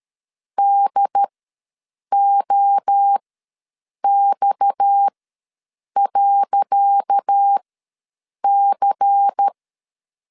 Morse code, heard as a keyed tone, about 13 wpm, spaced as for 7 wpm, 800 Hz. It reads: DOX.C
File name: LEZIONI / 10 SIGNS / 10C DOT ws 12wpm 3548